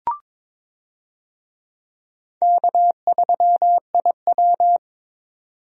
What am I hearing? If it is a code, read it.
Morse code, heard as a keyed tone, about 22 wpm, 700 Hz. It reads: K3IW
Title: USA Call Signs - 4 Characters - Repeated 3x - 22wpm